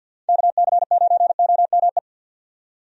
Morse code, heard as a keyed tone, about 52 wpm, 700 Hz. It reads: KC9OME
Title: USA Call Signs - 6 Characters - 50wpm